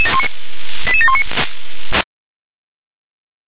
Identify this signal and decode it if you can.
Five-tone selective call (ZVEI), heard as quick runs of five tones, tones of about 80 ms. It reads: 88719